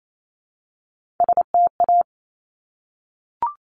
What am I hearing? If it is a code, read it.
Morse code, heard as a keyed tone, about 28 wpm, 700 Hz. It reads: STA
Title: Top Three Letter Combinations in Words 28wpm